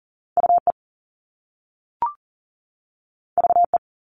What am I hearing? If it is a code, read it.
Morse code, heard as a keyed tone, about 40 wpm, 700 Hz. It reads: UE VE
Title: All Cut Numbers - 2-Digit 40wpm